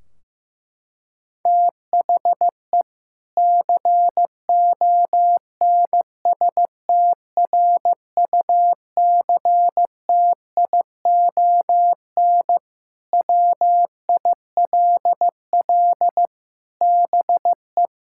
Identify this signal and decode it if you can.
Morse code, heard as a keyed tone, about 15 wpm, 700 Hz. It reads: THE CONSTRUCTION WILL BE
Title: Sentences from Top 1600 Words 15wpm